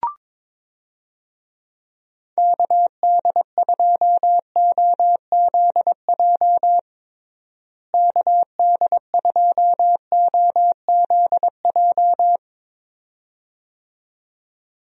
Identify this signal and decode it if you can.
Morse code, heard as a keyed tone, about 22 wpm, 700 Hz. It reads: KD2OZJ KD2OZJ
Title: USA Call Signs - 6 Characters - Repeated 2x - 22wpm